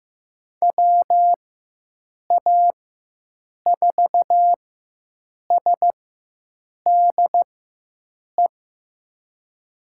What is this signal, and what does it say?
Morse code, heard as a keyed tone, about 15 wpm, spaced as for 7 wpm, 700 Hz. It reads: WA4SDE
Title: USA Call Signs - 6 Characters - 4x Character Spacing - 15wpm